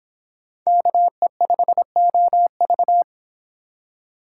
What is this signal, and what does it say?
Morse code, heard as a keyed tone, about 26 wpm, 700 Hz. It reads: KE5OV